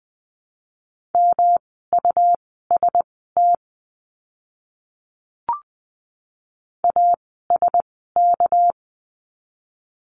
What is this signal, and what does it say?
Morse code, heard as a keyed tone, about 20 wpm, spaced as for 14 wpm, 700 Hz. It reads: MUST ASK
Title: 237 Lesson 38f - Letter q - Words 20wpm